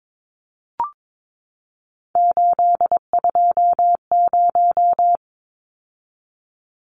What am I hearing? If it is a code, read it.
Morse code, heard as a keyed tone, about 22 wpm, 700 Hz. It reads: 820
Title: Numbers - 3-Digit 22wpm